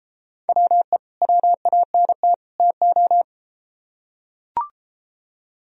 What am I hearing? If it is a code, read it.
Morse code, heard as a keyed tone, about 33 wpm, 700 Hz. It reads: WE WANT TO